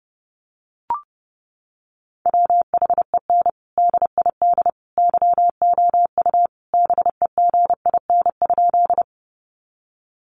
Morse code, heard as a keyed tone, about 30 wpm, 700 Hz. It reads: WHEN DID YOU BEGIN?
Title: Sentences from Top 300 Words 30wpm